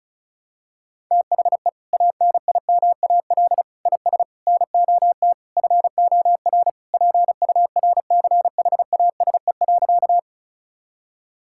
Morse code, heard as a keyed tone, about 36 wpm, 700 Hz. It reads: THE ANIMAL IS NOT FOR PURCHASE.
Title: Sentences from Top 1000 Words - Review Part 5 - 35wpm